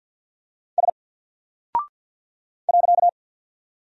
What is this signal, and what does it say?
Morse code, heard as a keyed tone, about 50 wpm, 700 Hz. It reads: S .
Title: Single Character (Letter:Number:Punctuation:Prosign) 50wpm